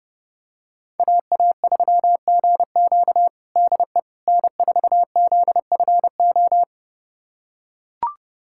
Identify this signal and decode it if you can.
Morse code, heard as a keyed tone, about 30 wpm, 700 Hz. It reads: AA3GQ DE N4ZFO
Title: USA Callsign DE Callsign - 5 Characters - 30wpm